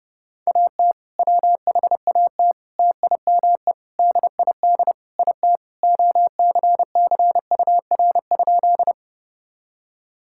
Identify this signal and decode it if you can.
Morse code, heard as a keyed tone, about 30 wpm, 700 Hz. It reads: AT WHAT TIME DID IT OCCUR?
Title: Sentences from Top 1000 Words - Review Part 6 - 30wpm